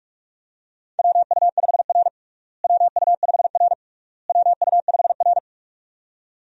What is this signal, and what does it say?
Morse code, heard as a keyed tone, about 42 wpm, 700 Hz. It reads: WU5R WU5R WU5R
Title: USA Call Signs - 4 Characters - Repeated 3x - 45wpm